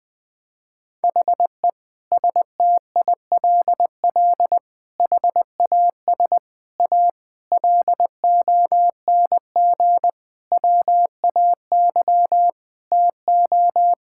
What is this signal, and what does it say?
Morse code, heard as a keyed tone, about 20 wpm, 700 Hz. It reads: HE STILL HAS A LONG WAY TO
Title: Sentences with 7-Word Phrases - Spoken Beginning - 20wpm